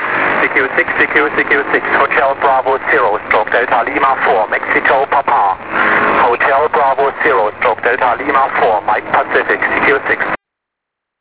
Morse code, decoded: none